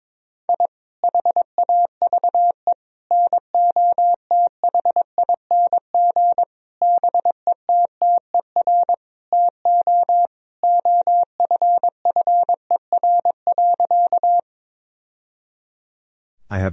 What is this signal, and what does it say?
Morse code, heard as a keyed tone, about 22 wpm, 700 Hz. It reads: I HAVE NOTHING BETTER TO OFFER.
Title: Sentences from Top 1000 Words - Review Part 4 - 22wpm